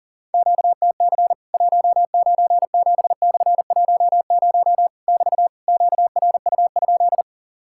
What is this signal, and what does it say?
Morse code, heard as a keyed tone, about 40 wpm, 700 Hz. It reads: QTC 197/10 <BT> QRU?
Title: Worked All Europe DX Contest - QTC Exchange - 40wpm